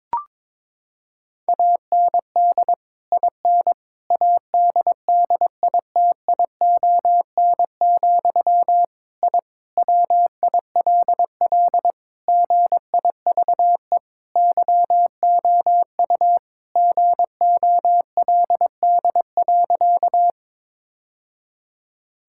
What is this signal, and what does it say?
Morse code, heard as a keyed tone, about 22 wpm, 700 Hz. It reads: AND IN ADDITION, I WILL GIVE YOU GOLD.
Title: Sentences from Top 1000 Words - Review Part 6 - 22wpm